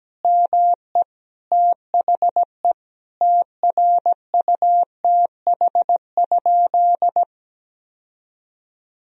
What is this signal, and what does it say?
Morse code, heard as a keyed tone, about 17 wpm, 700 Hz. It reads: ME THE TRUTH?
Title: Sentences from Top 1000 Words - Review Part 3 - 17wpm